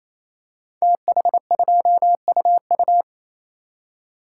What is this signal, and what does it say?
Morse code, heard as a keyed tone, about 27 wpm, 700 Hz. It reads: TH2UU
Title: Code Groups - 5 Letter:Numbers - 28wpm